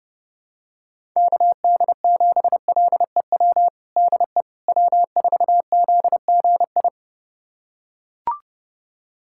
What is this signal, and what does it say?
Morse code, heard as a keyed tone, about 30 wpm, 700 Hz. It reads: KD7LEW DE W4ZGI
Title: USA Callsign DE Callsign 30wpm